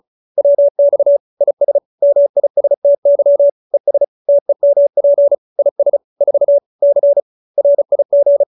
Morse code, heard as a keyed tone, about 34 wpm, 570 Hz. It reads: WX IS MISTY ES TEMP IS 4 C RIG